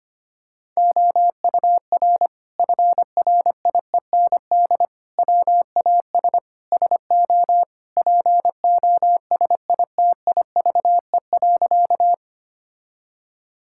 Morse code, heard as a keyed tone, about 25 wpm, 700 Hz. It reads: OUR FRIEND WAS SO POSITIVE.
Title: Sentences from Top 1000 Words - Review Part 5 - 25wpm